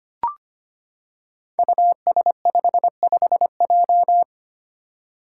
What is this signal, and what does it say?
Morse code, heard as a keyed tone, about 25 wpm, 700 Hz. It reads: US55J
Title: Five Challenging Letter-Number Combinations 25wpm